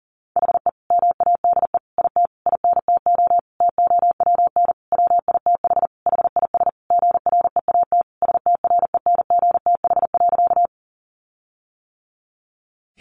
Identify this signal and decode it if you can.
Morse code, heard as a keyed tone, about 40 wpm, 700 Hz. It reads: HE MADE IT INTO TOWN WITH HIS GREAT STRENGTH.
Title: Sentences from Top 1300 Words 40wpm